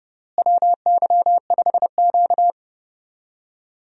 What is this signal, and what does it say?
Morse code, heard as a keyed tone, about 29 wpm, 700 Hz. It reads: WY5Q
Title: USA Call Signs - 4 Characters - Repeated 2x - 30wpm